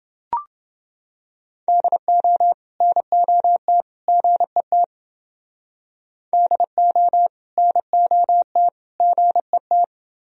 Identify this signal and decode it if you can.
Morse code, heard as a keyed tone, about 28 wpm, 700 Hz. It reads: DO NOT GET DO NOT GET